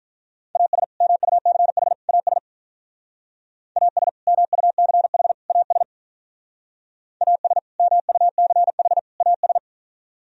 Morse code, heard as a keyed tone, about 42 wpm, 700 Hz. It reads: AS MUCH AS AS MUCH AS AS MUCH AS
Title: Speed-Racing - Sets of 3 Words 35wpm